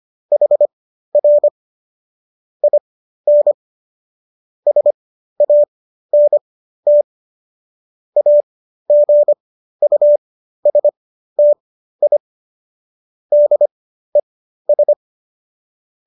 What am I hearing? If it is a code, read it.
Morse code, heard as a keyed tone, about 25 wpm, spaced as for 13 wpm, 590 Hz. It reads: HR IN SANT AGUSTI DES
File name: qso214 13